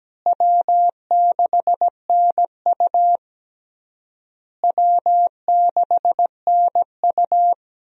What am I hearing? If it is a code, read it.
Morse code, heard as a keyed tone, about 17 wpm, 700 Hz. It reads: W6NU W6NU
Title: USA Call Signs - 4 Characters - Repeated 3x - 17wpm